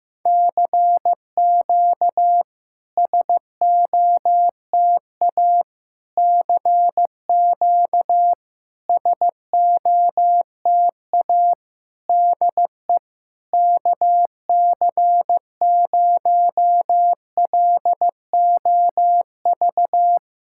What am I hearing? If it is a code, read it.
Morse code, heard as a keyed tone, about 15 wpm, 700 Hz. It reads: CQ SOTA CQ SOTA DE KC0LOV
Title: SOTA QSO Practice 15wpm